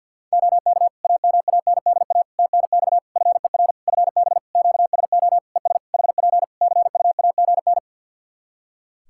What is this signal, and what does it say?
Morse code, heard as a keyed tone, about 52 wpm, 700 Hz. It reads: OK AMANDA TNX FER FB QSO ES HP CUAGN
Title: K1USN SST QSO Practice 50wpm